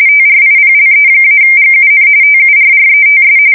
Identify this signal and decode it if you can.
RTTY, about 45.45 baud. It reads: TRIPLE-GUN TURRETS